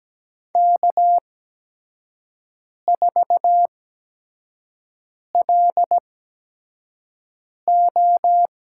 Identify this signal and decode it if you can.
Morse code, heard as a keyed tone, about 17 wpm, spaced as for 5 wpm, 700 Hz. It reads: K4LO
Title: USA Call Signs - 4 Characters - 8x Character Spacing - 17wpm